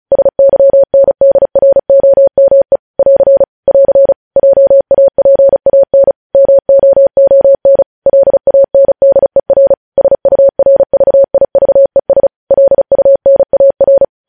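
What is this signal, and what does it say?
Morse code, heard as a keyed tone, about 36 wpm, 550 Hz. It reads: SYNDROME <AR> <AR> JAPAN MOON LANDER SURVIVES LUNAR
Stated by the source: Morse practice sending news headlines at 35 wpm.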